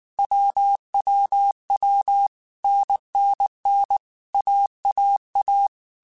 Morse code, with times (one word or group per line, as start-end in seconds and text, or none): 0.2-2.3 WWW
2.6-4.0 NNN
4.3-5.7 AAA